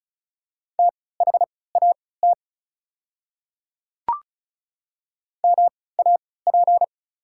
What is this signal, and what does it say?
Morse code, heard as a keyed tone, about 34 wpm, spaced as for 20 wpm, 700 Hz. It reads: THAT MAP
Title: 3x Character Spacing - Top 500 Words - 35wpm